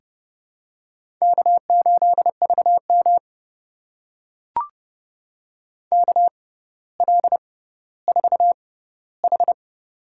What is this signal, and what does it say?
Morse code, heard as a keyed tone, about 28 wpm, spaced as for 15 wpm, 700 Hz. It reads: K8VM KL4H
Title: USA Call Signs - 4 Characters - 6x Character Spacing - 30wpm